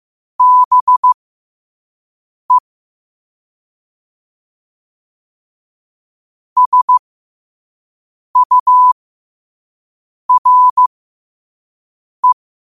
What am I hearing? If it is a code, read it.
Morse code, heard as a keyed tone, about 15 wpm, spaced as for 5 wpm, 1000 Hz. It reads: BE SURE
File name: qotdfiles / 5wpm